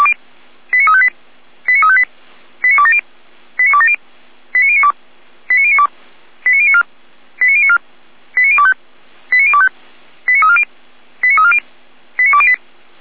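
Five-tone selective call (ZVEI), heard as quick runs of five tones, tones of about 70 ms. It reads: 89358 89358 89270 89270 89092 89092 89094 89094 89025 89025 89340 89340 89208
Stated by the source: ZVEI five-tone selective call received by SDR.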